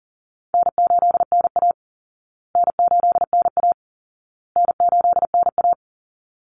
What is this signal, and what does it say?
Morse code, heard as a keyed tone, about 40 wpm, 700 Hz. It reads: N8NA N8NA N8NA
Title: Call Signs Repeated 2x - 40wpm